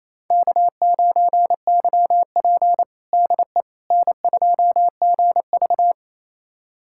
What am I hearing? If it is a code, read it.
Morse code, heard as a keyed tone, about 28 wpm, 700 Hz. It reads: K9YP DE N2GV
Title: USA Callsign DE Callsign - 4 Characters - 28wpm